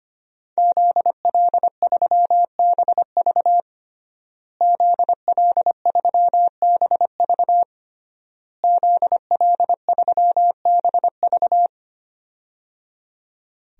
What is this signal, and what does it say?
Morse code, heard as a keyed tone, about 25 wpm, 700 Hz. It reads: ZL3BV ZL3BV ZL3BV
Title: DXCC-100 Call Signs - 5 Characters - Repeated 3x - 25wpm